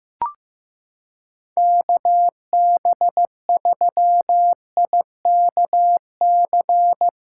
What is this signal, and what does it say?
Morse code, heard as a keyed tone, about 15 wpm, 700 Hz. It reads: KB3IKC
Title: USA Callsign DE Callsign - 6 Characters - 15wpm